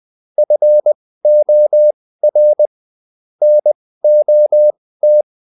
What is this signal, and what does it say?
Morse code, heard as a keyed tone, about 20 wpm, spaced as for 15 wpm, 600 Hz. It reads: FOR NOT
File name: Top-100-Words-15